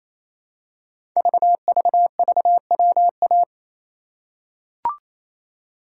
Morse code, heard as a keyed tone, about 27 wpm, 700 Hz. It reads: VVVWA
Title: Code Groups - 5 Letters - 28wpm